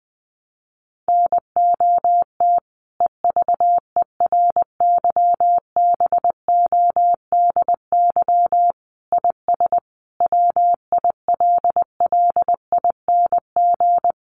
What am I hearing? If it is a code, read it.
Morse code, heard as a keyed tone, about 20 wpm, 700 Hz. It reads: NOT EVERYBODY IS WILLING
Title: Sentences from Top 1400 Words 20wpm